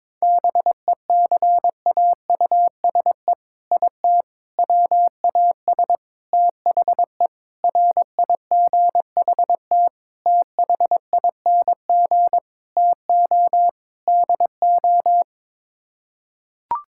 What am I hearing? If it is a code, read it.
Morse code, heard as a keyed tone, about 22 wpm, 700 Hz. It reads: BECAUSE IT WAS THE RIGHT THING TO DO
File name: practice / 22wpm / Sets of 8 Words - 22wpm